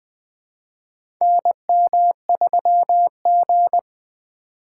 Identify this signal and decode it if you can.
Morse code, heard as a keyed tone, about 21 wpm, 700 Hz. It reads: NM3G
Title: USA Call Signs - 4 Characters - 4x Character Spacing - 20wpm